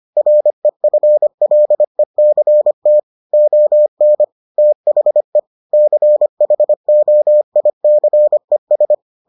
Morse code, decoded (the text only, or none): REFLECT ON THE CHOICES